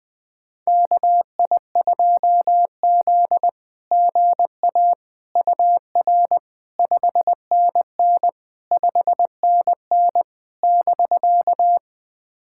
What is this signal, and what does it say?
Morse code, heard as a keyed tone, about 21 wpm, 700 Hz. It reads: KI2Z GA UR 5NN 5NN <BK>
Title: SOTA QSO Practice 20wpm